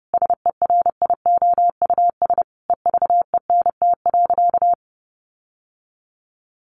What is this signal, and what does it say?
Morse code, heard as a keyed tone, about 30 wpm, 700 Hz. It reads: SERIOUS EVENT.
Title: Sentences from Top 1400 Words 30wpm